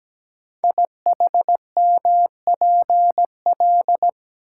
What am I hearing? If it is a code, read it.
Morse code, heard as a keyed tone, about 17 wpm, 700 Hz. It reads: IHMPL
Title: Code Groups - 5 Letters - 17wpm